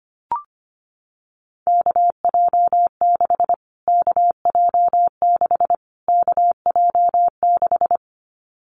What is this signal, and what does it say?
Morse code, heard as a keyed tone, about 25 wpm, 700 Hz. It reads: KJ6 KJ6 KJ6
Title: Callsign Prefixes for Top 25 DXCC - Repeated 3x - 25wpm